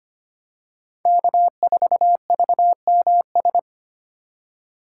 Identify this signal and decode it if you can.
Morse code, heard as a keyed tone, about 25 wpm, 700 Hz. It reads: K4VMS
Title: USA Call Signs - 4x Character Spacing - 25wpm